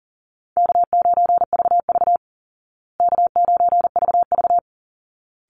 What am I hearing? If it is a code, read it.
Morse code, heard as a keyed tone, about 37 wpm, 700 Hz. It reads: K9VV K9VV
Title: Call Signs Repeated 3x - 40wpm